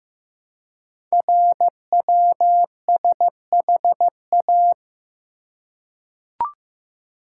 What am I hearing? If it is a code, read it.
Morse code, heard as a keyed tone, about 15 wpm, 690 Hz. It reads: RWSHA
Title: Code Groups - 5 Letters - 15wpm